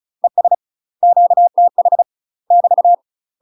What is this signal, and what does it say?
Morse code, heard as a keyed tone, about 34 wpm, 700 Hz. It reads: ES QTH <BT>
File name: UK MASTER QSOs TRAINING / 35WPM / UK MASTER QSOs 0003